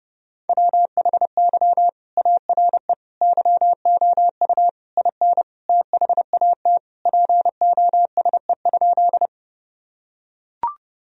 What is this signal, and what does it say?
Morse code, heard as a keyed tone, about 32 wpm, 700 Hz. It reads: WHY ARE YOU IN THAT POSE?
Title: Sentences from Top 1000 Words - Review Part 2 - 30wpm